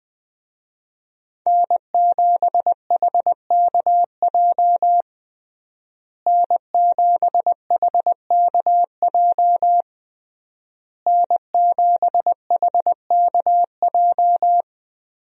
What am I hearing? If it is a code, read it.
Morse code, heard as a keyed tone, about 20 wpm, 700 Hz. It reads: N7HKJ N7HKJ N7HKJ
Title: USA Call Signs - 5 Characters - Repeated 3x - 20wpm